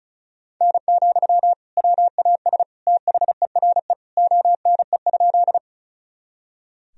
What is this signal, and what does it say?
Morse code, heard as a keyed tone, about 34 wpm, 700 Hz. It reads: N, WAS THERE ONE?
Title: Sentences from Top 100 Words and Abbreviations 35wpm